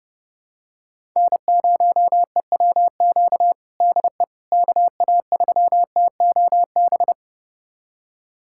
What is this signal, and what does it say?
Morse code, heard as a keyed tone, about 32 wpm, 700 Hz. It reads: N0EWQ DE KA3TOB
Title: USA Callsign DE Callsign 30wpm